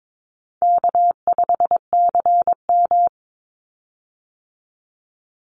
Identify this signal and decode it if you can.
Morse code, heard as a keyed tone, about 22 wpm, 700 Hz. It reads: K5CM
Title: Call Signs Repeated 3x - 22wpm